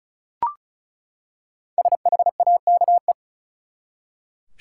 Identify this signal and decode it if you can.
Morse code, heard as a keyed tone, about 34 wpm, 700 Hz. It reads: SHAKE
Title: Five Letter Words - 35wpm